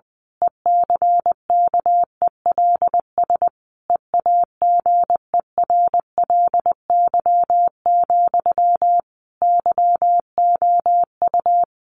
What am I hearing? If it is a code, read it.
Morse code, heard as a keyed tone, about 20 wpm, 700 Hz. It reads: ECKELS EAGERLY, YOU
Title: A Sound Of Thunder 20wpm